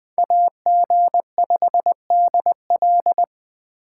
Morse code, heard as a keyed tone, about 20 wpm, 700 Hz. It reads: AG5DL